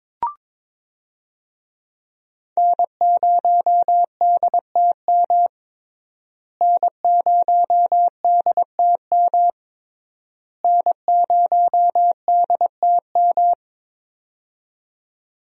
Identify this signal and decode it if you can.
Morse code, heard as a keyed tone, about 22 wpm, 700 Hz. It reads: N0DTM N0DTM N0DTM
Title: USA Call Signs - 5 Characters - Repeated 3x - 22wpm